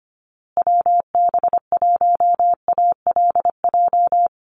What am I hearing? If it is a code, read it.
Morse code, heard as a keyed tone, about 25 wpm, 700 Hz. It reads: WB1ALJ
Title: USA Call Signs Repeated 2x 25wpm